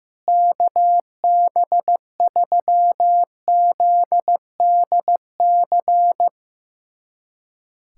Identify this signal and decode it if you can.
Morse code, heard as a keyed tone, about 15 wpm, 700 Hz. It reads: KB3ZDC
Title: USA Callsign DE Callsign - 6 Characters - 15wpm